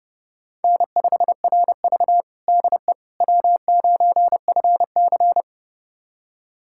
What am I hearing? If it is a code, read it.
Morse code, heard as a keyed tone, about 30 wpm, 700 Hz. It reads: N5RV DE W9FC
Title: USA Callsign DE Callsign - 4 Characters - 30wpm